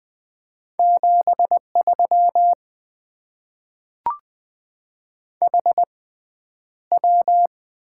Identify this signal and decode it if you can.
Morse code, heard as a keyed tone, about 20 wpm, spaced as for 10 wpm, 700 Hz. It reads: 73 HW
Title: Top 20 QSO Elements - 6x Character Spacing - 20wpm